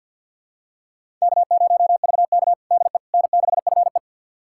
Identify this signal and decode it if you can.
Morse code, heard as a keyed tone, about 52 wpm, 700 Hz. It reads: K0VK DE N6FE
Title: USA Callsign DE Callsign - 4 Characters - 50wpm